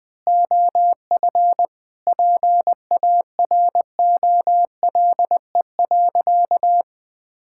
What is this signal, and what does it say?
Morse code, heard as a keyed tone, about 20 wpm, 700 Hz. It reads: OF PAROLE.
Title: Sentences with 8-Word Phrases - 20wpm